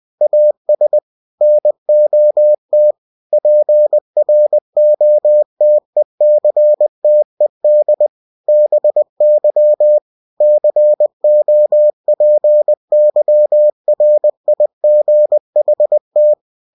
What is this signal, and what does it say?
Morse code, heard as a keyed tone, about 20 wpm, 600 Hz. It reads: AS NOT PROTECTED BY COPYRIGHT
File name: cw-books / winnie-the-pooh / winnie-20wpm-page-0122